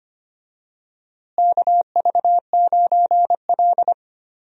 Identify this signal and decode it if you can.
Morse code, heard as a keyed tone, about 25 wpm, 700 Hz. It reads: KV9L
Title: USA Call Signs - 4 Characters - Repeated 2x - 25wpm